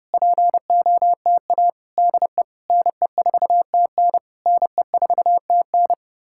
Morse code, heard as a keyed tone, about 33 wpm, 700 Hz. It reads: POTA DE NE4TN NE4TN